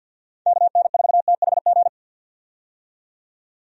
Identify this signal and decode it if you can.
Morse code, heard as a keyed tone, about 51 wpm, 700 Hz. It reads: KN4THG
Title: USA Call Signs - 6 Characters - 50wpm